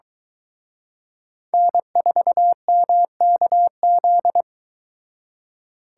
Morse code, heard as a keyed tone, about 23 wpm, 700 Hz. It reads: N4MKZ